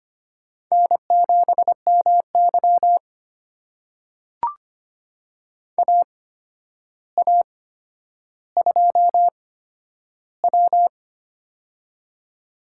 Morse code, heard as a keyed tone, about 25 wpm, spaced as for 11 wpm, 700 Hz. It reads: N7MY AA2W
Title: USA Call Signs - 4 Characters - 8x Character Spacing - 25wpm